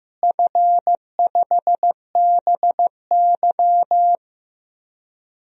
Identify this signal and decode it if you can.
Morse code, heard as a keyed tone, about 15 wpm, 700 Hz. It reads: F5BY
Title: DXCC-100 Call Signs - 4 Characters - Repeated 3x - 15wpm